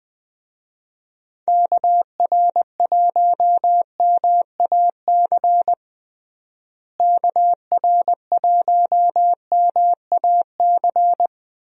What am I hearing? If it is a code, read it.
Morse code, heard as a keyed tone, about 20 wpm, 700 Hz. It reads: KR1MAC KR1MAC
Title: USA Call Signs - 6 Characters - Repeated 3x - 20wpm